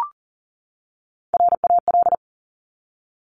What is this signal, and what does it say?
Morse code, heard as a keyed tone, about 42 wpm, 690 Hz. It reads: RAL
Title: Top Three Letter Combinations in Words 40wpm